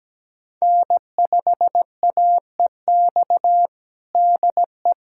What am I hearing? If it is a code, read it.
Morse code, heard as a keyed tone, about 17 wpm, 700 Hz. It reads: N5AEX DE
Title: USA Callsign DE Callsign - 5 Characters - 17wpm